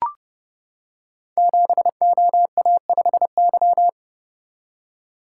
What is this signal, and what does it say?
Morse code, heard as a keyed tone, about 30 wpm, 700 Hz. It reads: 7OA5Y
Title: Code Groups - 5 Letter:Numbers - 30wpm